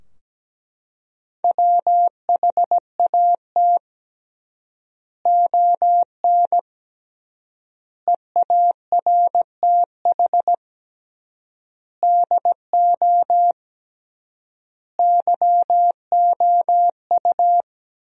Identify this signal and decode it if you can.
Morse code, heard as a keyed tone, about 17 wpm, 700 Hz. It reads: WHAT ON EARTH DO YOU
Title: 3x Word-Spacing - Sentences from Top 200 Words - 17wpm